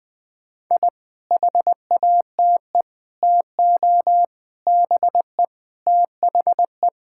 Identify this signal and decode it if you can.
Morse code, heard as a keyed tone, about 21 wpm, 700 Hz. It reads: I HATE TO BE THE
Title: Sentences with 9-Word Phrases - 20wpm